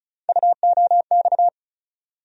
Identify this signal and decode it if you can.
Morse code, heard as a keyed tone, about 36 wpm, 700 Hz. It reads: UOX
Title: USA Call Sign Postfixes - 3 Letters 35wpm